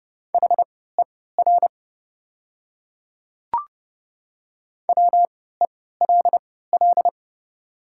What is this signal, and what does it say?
Morse code, heard as a keyed tone, about 30 wpm, spaced as for 17 wpm, 700 Hz. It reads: HER WELL
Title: 3x Character Spacing - Top 200 Words - 30wpm